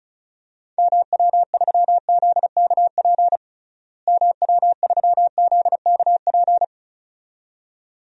Morse code, heard as a keyed tone, about 34 wpm, 700 Hz. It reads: MW3ZKP MW3ZKP
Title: DXCC-100 Call Signs - 6 Characters - Repeated 2x - 35wpm